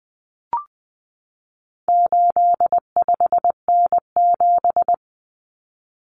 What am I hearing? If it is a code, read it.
Morse code, heard as a keyed tone, about 20 wpm, 700 Hz. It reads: 85N7
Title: Common Cut Numbers - 4-Digit 20wpm